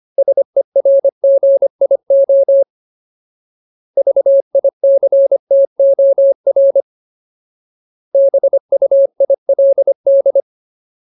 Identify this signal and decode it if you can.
Morse code, heard as a keyed tone, about 25 wpm, 550 Hz. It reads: SERGIO VICTOR BUILD
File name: Ultimate Word List / 25WPM Ultimate